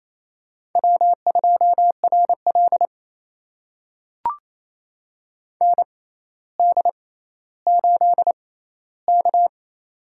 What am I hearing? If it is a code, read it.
Morse code, heard as a keyed tone, about 28 wpm, spaced as for 14 wpm, 700 Hz. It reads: W2RL ND8K
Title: USA Call Signs - 4 Characters - 6x Character Spacing - 28wpm